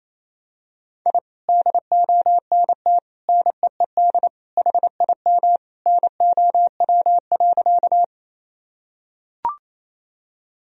Morse code, heard as a keyed tone, about 28 wpm, 700 Hz. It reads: I DONT NEED HIM NOW.
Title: Sentences from Top 1000 Words - Review Part 1 - 28wpm